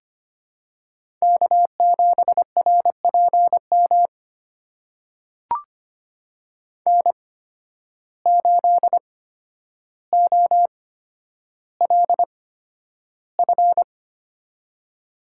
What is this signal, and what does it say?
Morse code, heard as a keyed tone, about 25 wpm, spaced as for 11 wpm, 700 Hz. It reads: K7RPM N8OLF
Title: USA Call Signs - 5 Characters - 8x Character Spacing - 25wpm